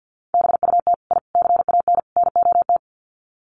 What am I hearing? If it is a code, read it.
Morse code, heard as a keyed tone, about 50 wpm, 700 Hz. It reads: BUT I CAN NOT
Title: Sets of 4 Words 50wpm